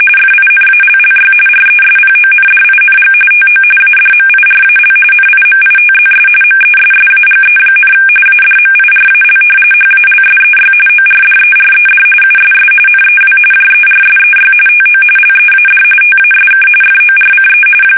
RTTY, about 45.45 baud. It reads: FINALLY RESPONDING TO ADMIRAL NIMITZS MESSAGE, ADMIRAL HALSEY ORDERED USS NEW JERSEY TO LEAD TASK FORCE
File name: fsk850 / battleshipnj